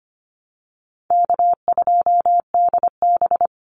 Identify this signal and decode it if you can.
Morse code, heard as a keyed tone, about 25 wpm, 700 Hz. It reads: K2DB
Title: Easy Call Signs - 25wpm